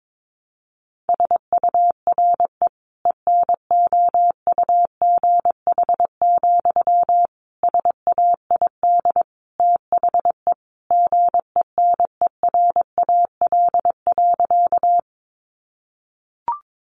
Morse code, heard as a keyed tone, about 22 wpm, 700 Hz. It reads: SURE ENOUGH, SAID THE GENERAL.
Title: Sentences from Top 600 Words 22wpm